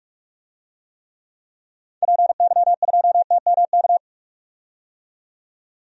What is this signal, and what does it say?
Morse code, heard as a keyed tone, about 44 wpm, 700 Hz. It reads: PY2TMK
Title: DXCC-100 Call Signs - 6 Characters - 45wpm